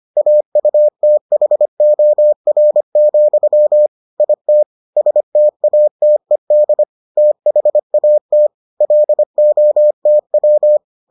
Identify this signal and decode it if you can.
Morse code, heard as a keyed tone, about 25 wpm, 600 Hz. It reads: AUTHOR, IT STATED THAT LOTW